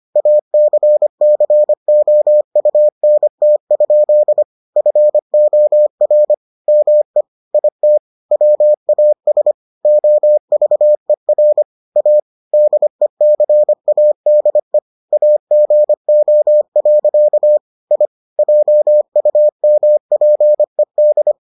Morse code, heard as a keyed tone, about 25 wpm, 600 Hz. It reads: ACCOUNT? FOR ME IT WAS OVER A DECADE AGO. I JUMPED